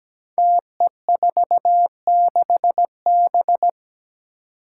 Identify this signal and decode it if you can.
Morse code, heard as a keyed tone, about 17 wpm, 700 Hz. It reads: TE46B
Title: Five Challenging Letter-Number Combinations 17wpm